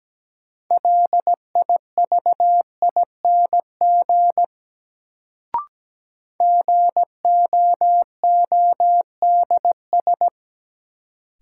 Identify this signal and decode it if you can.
Morse code, heard as a keyed tone, about 17 wpm, 700 Hz. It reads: LIVING GOODS
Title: Sentences from Top 1000 Words - Review Part 6 - New Word Drill - 17wpm